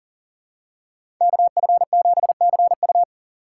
Morse code, heard as a keyed tone, about 40 wpm, 700 Hz. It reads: KF7CU
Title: USA Call Signs - 2x Character Spacing - 40wpm